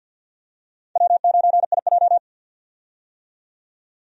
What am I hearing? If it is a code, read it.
Morse code, heard as a keyed tone, about 53 wpm, 700 Hz. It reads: W9IJ
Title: USA Call Signs - 4 Characters - 50wpm